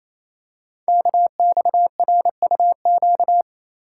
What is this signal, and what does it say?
Morse code, heard as a keyed tone, about 27 wpm, 700 Hz. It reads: KXRUQ
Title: Code Groups - 5 Letters - 28wpm